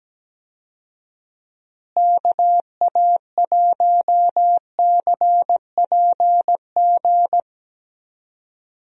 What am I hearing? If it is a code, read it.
Morse code, heard as a keyed tone, about 17 wpm, 700 Hz. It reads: KA1CPG
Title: USA Call Signs - 6 Characters - Repeated 2x - 17wpm